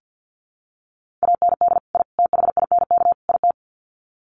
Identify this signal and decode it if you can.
Morse code, heard as a keyed tone, about 50 wpm, 700 Hz. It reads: AND I THINK IT
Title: Sets of 4 Words 50wpm